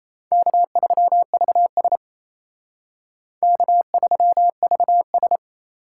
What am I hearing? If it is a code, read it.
Morse code, heard as a keyed tone, about 32 wpm, 700 Hz. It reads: K3VS K3VS
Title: Speed Racing - USA 1x2 Call Signs 28wpm